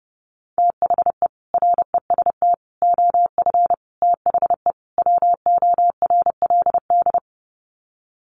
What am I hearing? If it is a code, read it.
Morse code, heard as a keyed tone, about 30 wpm, 700 Hz. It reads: THE REST OF THE WORLD
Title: Sets of 5 Words 30wpm